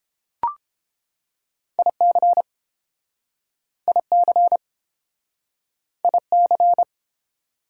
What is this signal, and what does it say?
Morse code, heard as a keyed tone, about 31 wpm, 700 Hz. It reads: IC IC IC